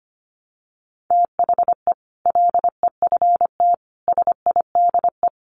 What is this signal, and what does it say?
Morse code, heard as a keyed tone, about 25 wpm, 700 Hz. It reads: THE LEFT SIDE
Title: Sets of 5 Words Encore 25wpm